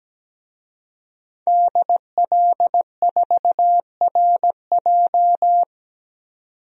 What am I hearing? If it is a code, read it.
Morse code, heard as a keyed tone, about 17 wpm, 700 Hz. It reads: DL4RJ
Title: DXCC-100 Call Signs - 5 Characters - 17wpm